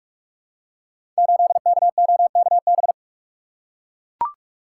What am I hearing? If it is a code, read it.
Morse code, heard as a keyed tone, about 44 wpm, 700 Hz. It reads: 8KOKB
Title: Code Groups - 5 Letter:Numbers - 45wpm